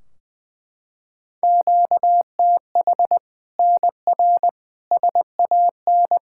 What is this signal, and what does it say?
Morse code, heard as a keyed tone, about 21 wpm, 700 Hz. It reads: QTH NR SAN
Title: Practice QSO - Exchange 1 - Beginning - 20wpm